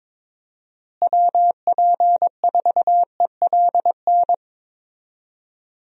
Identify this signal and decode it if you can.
Morse code, heard as a keyed tone, about 22 wpm, 700 Hz. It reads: WP4ELN
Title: USA Call Signs - 6 Characters - Repeated 3x - 22wpm